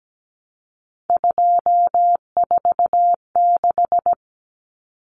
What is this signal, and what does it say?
Morse code, heard as a keyed tone, about 17 wpm, 700 Hz. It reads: 246
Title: Common Cut Numbers - 3-Digit 17wpm